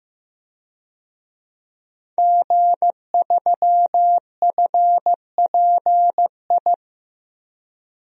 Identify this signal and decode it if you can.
Morse code, heard as a keyed tone, about 16 wpm, 700 Hz. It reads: G3FPI
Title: DXCC-100 Call Signs - 5 Characters - Repeated 2x - 15wpm